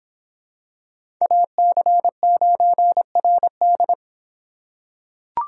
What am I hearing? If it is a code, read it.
Morse code, heard as a keyed tone, about 25 wpm, 700 Hz. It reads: AC9RD